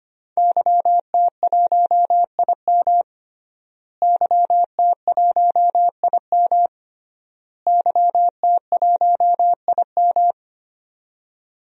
Morse code, heard as a keyed tone, about 25 wpm, 700 Hz. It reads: YT1IM YT1IM YT1IM
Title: DXCC-100 Call Signs - 5 Characters - Repeated 3x - 25wpm